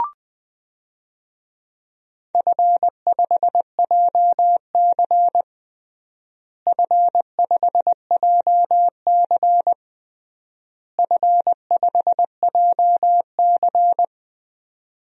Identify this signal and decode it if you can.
Morse code, heard as a keyed tone, about 20 wpm, 700 Hz. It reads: F5JC F5JC F5JC
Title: DXCC-100 Call Signs - 4 Characters - Repeated 3x - 20wpm